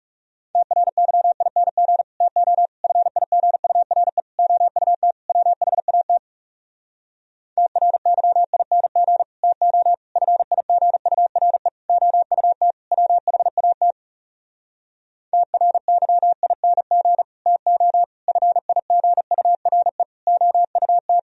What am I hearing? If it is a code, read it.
Morse code, heard as a keyed tone, about 40 wpm, 700 Hz. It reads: TRYING TO FIGURE OUT WHAT TRYING TO FIGURE OUT WHAT TRYING TO FIGURE OUT